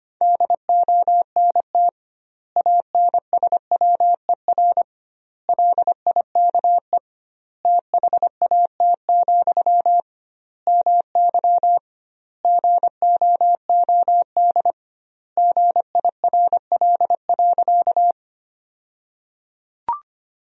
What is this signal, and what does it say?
Morse code, heard as a keyed tone, about 25 wpm, 700 Hz. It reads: DONT ANSWER LIKE THAT, MY GOOD GIRL.
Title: 2x Word-Spacing - Sentences from Top 400 Words - 25wpm